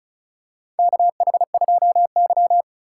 Morse code, heard as a keyed tone, about 36 wpm, 700 Hz. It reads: KH2Y
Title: USA Call Signs - 3x Character Spacing - 35wpm